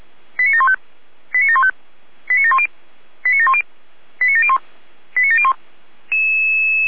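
Five-tone selective call (ZVEI), heard as quick runs of five tones, tones of about 70 ms. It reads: 89715 89715 89710 89710 89701 89701
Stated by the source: ZVEI five-tone call 89490 received by SDR.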